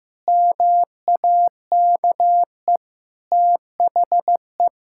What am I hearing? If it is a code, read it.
Morse code, heard as a keyed tone, about 15 wpm, 700 Hz. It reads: MAKE THE
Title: Sentences from Top 1000 Words - Review Part 6 - 15wpm